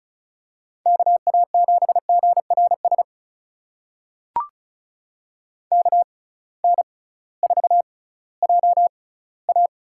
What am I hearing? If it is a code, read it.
Morse code, heard as a keyed tone, about 36 wpm, spaced as for 19 wpm, 700 Hz. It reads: KA7GRS KN4JA
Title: USA Call Signs - 6x Character Spacing - 35wpm